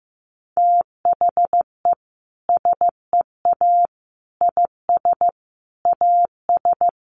Morse code, heard as a keyed tone, about 15 wpm, 700 Hz. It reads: THE SEA IS AS
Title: Sentences from Top 1400 Words 15wpm